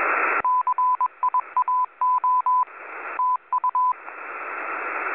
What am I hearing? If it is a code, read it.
Morse code, heard as a keyed tone, about 21 wpm, 1000 Hz, with radio noise. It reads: CIAO TU
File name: SSTV images / websdr recording start 14230.0kHz